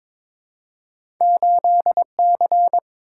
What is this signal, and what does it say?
Morse code, heard as a keyed tone, about 22 wpm, 700 Hz. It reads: 8C
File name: practice / 22wpm / Two Letter-Number Combinations - 22wpm